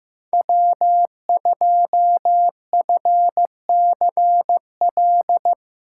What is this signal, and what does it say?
Morse code, heard as a keyed tone, about 16 wpm, 700 Hz. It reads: W2FCL
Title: USA Callsign DE Callsign 15wpm